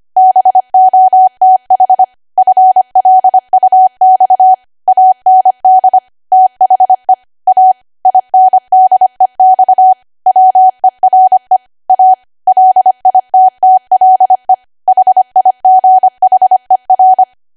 Morse code, heard as a keyed tone, about 25 wpm, 750 Hz. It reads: BOTH FLUX AND THE A INDEX WERE A LITTLE HIGHER
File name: cwa / W1AW-Level3 / 25WPM